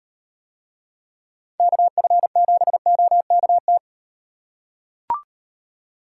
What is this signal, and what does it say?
Morse code, heard as a keyed tone, about 38 wpm, 700 Hz. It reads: KF7OKT